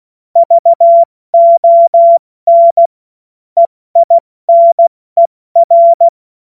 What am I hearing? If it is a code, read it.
Morse code, heard as a keyed tone, about 16 wpm, 680 Hz. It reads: VON EINER
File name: Morsekurs / Witze / Witz 02 April 2024 70 F80